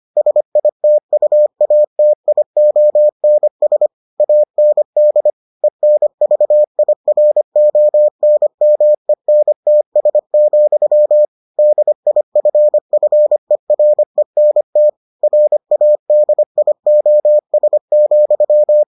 Morse code, heard as a keyed tone, about 25 wpm, 600 Hz. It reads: SITUATIONS AND ENVIRONMENTS, DIFFERENT RADIOS,